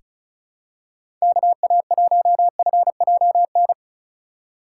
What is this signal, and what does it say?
Morse code, heard as a keyed tone, about 36 wpm, 700 Hz. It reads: KA1FJN
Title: USA Call Signs - 6 Characters - 4x Character Spacing - 35wpm